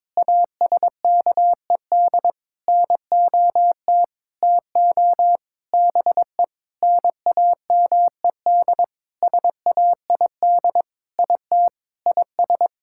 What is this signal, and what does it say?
Morse code, heard as a keyed tone, about 22 wpm, 700 Hz. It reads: ASKED NOT TO BE NAMED SAID IT IS
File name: Sentences with 7-Word Phrases - 22wpm